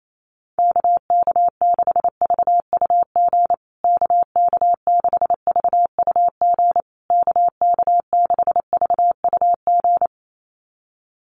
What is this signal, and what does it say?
Morse code, heard as a keyed tone, about 28 wpm, 700 Hz. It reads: KK6VUG KK6VUG KK6VUG
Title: USA Call Signs Repeated 3x 28wpm